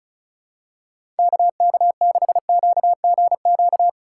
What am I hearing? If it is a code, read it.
Morse code, heard as a keyed tone, about 36 wpm, 700 Hz. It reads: KK6QGQ
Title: USA Call Signs - 6 Characters - 3x Character Spacing - 35wpm